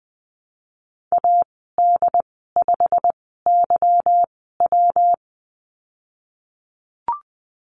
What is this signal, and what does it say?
Morse code, heard as a keyed tone, about 20 wpm, spaced as for 15 wpm, 700 Hz. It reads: AD5YW
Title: 147 Lesson 24g - Letter y - Call Signs 20wpm